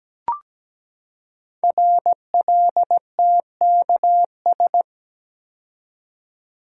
Morse code, heard as a keyed tone, about 17 wpm, 700 Hz. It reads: RLTKS
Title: Code Groups - 5 Letters - 17wpm